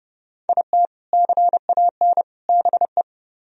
Morse code, heard as a keyed tone, about 39 wpm, 700 Hz. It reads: IT CAN BE